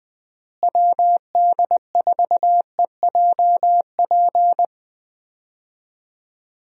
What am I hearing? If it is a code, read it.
Morse code, heard as a keyed tone, about 20 wpm, 700 Hz. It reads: WD4EJP
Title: USA Call Signs - 6 Characters - Repeated 3x - 20wpm